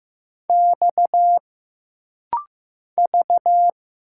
Morse code, heard as a keyed tone, about 15 wpm, 700 Hz. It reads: X V
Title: Single Letter-Number Alt - Rapid-Fire 15wpm